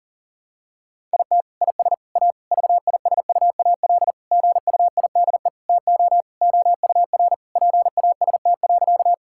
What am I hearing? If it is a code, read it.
Morse code, heard as a keyed tone, about 40 wpm, 700 Hz. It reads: IT IS A VISUAL GUIDE TO OUR PAST.
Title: Sentences from Top 1000 Words - Review Part 6 - 40wpm